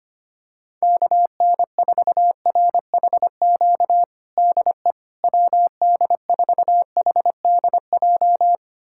Morse code, heard as a keyed tone, about 25 wpm, 700 Hz. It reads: KN4RHQ DE WD4HDJ
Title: USA Callsign DE Callsign - 6 Characters - 25wpm